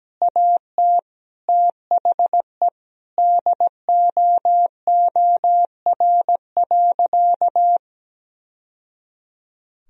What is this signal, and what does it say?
Morse code, heard as a keyed tone, about 17 wpm, 700 Hz. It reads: AT THE DOOR.
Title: Sentences from Top 1000 Words - Review Part 2 - 17wpm